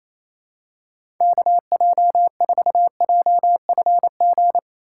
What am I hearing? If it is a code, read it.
Morse code, heard as a keyed tone, about 28 wpm, 700 Hz. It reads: KJ4JFG
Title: USA Call Signs - 6 Characters - 4x Character Spacing - 28wpm